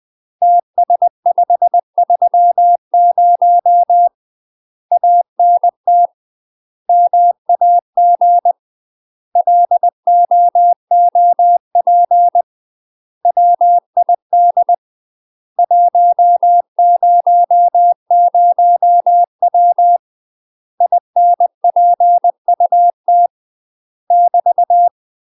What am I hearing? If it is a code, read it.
Morse code, heard as a keyed tone, about 20 wpm, 700 Hz. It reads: TS530 ANT MAG LOOP WID 100W INPUT <BT>
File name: UK MASTER QSOs TRAINING / 20WPM / UK MASTER QSOs 0060